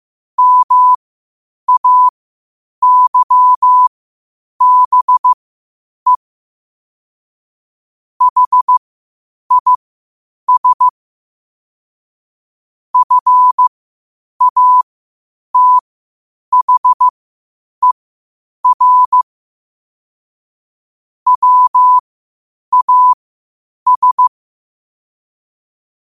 Morse code, heard as a keyed tone, about 15 wpm, spaced as for 8 wpm, 1000 Hz. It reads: MAYBE HIS FATHER WAS
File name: qotdfiles / 8wpm